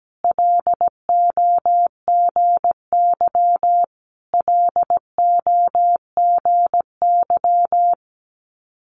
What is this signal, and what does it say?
Morse code, heard as a keyed tone, about 17 wpm, 700 Hz. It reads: LOGY LOGY
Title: Multi-letter Phonemes Repeated 3x 17wpm